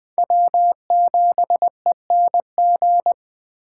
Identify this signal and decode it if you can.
Morse code, heard as a keyed tone, about 20 wpm, 700 Hz. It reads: W7ENG
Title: Speed Racing - USA 1x3 Call Signs 17wpm